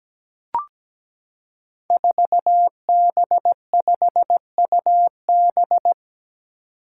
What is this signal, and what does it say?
Morse code, heard as a keyed tone, about 17 wpm, 700 Hz. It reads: 4B5UB
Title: Five Challenging Letter-Number Combinations 17wpm